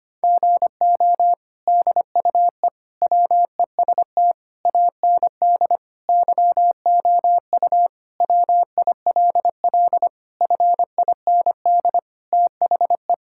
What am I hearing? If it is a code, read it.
Morse code, heard as a keyed tone, about 25 wpm, 700 Hz. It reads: GO DUE WEST AND YOU WILL FIND THE
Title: Sentences from Top 1000 Words - Review Part 6 - 25wpm